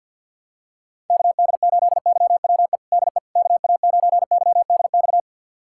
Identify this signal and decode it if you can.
Morse code, heard as a keyed tone, about 52 wpm, 700 Hz. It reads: KD8YWE DE KA9YDX
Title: USA Callsign DE Callsign 50wpm